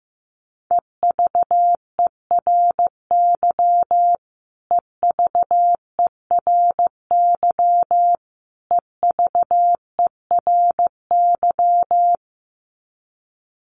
Morse code, heard as a keyed tone, about 15 wpm, 700 Hz. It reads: EVERY EVERY EVERY
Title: Top 500 Words Repeated 15wpm